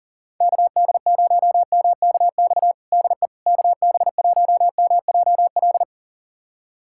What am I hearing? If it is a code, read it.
Morse code, heard as a keyed tone, about 40 wpm, 700 Hz. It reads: KD0MKX DE KB1MJL
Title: USA Callsign DE Callsign - 6 Characters - 40wpm